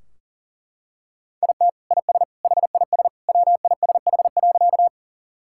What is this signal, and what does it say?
Morse code, heard as a keyed tone, about 40 wpm, 700 Hz. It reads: IT IS HIS WISH.
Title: Sentences from Top 1600 Words 40wpm